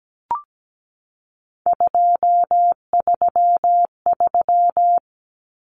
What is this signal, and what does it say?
Morse code, heard as a keyed tone, about 17 wpm, 700 Hz. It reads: 233
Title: Numbers - 3-Digit 17wpm